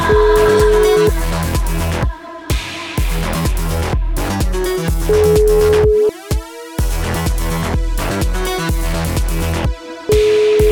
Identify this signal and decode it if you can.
Ringback tone (Europe, 425 Hz), with noise.